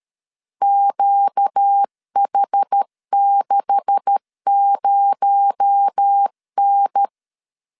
Morse code, heard as a keyed tone, about 13 wpm, 800 Hz. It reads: QH60N
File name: LEZIONI / 9 286 12wpm 2305